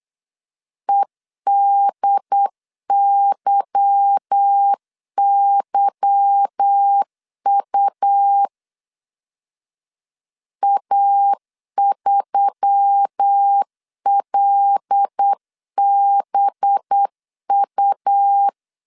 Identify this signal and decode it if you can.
Morse code, heard as a keyed tone, about 8 wpm, 800 Hz. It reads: EDYYU A3LBU
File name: LEZIONI / 8 YJ45 / 8 Y 8wpm 3245